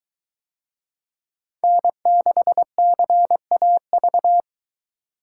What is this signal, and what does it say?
Morse code, heard as a keyed tone, about 22 wpm, 700 Hz. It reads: N6CAV